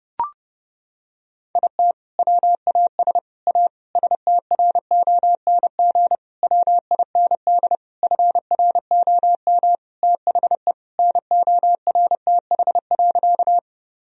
Morse code, heard as a keyed tone, about 30 wpm, 700 Hz. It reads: IT WAS A STRONG WIND FROM THE NORTH.
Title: Sentences from Top 1000 Words - Review Part 2 - 30wpm